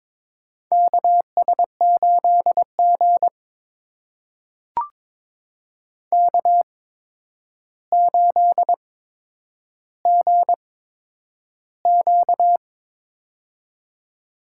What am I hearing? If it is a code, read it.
Morse code, heard as a keyed tone, about 22 wpm, spaced as for 9 wpm, 700 Hz. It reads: KS8G K8GQ
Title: USA Call Signs - 4 Characters - 8x Character Spacing - 22wpm